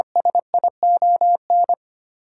Morse code, heard as a keyed tone, about 25 wpm, 700 Hz. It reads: SION
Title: Sentences from Top 1400 Words - New Word Drill - 25wpm